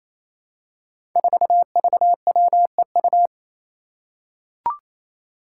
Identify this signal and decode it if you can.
Morse code, heard as a keyed tone, about 27 wpm, 700 Hz. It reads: 4VWEU
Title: Five Challenging Letter-Number Combinations 28wpm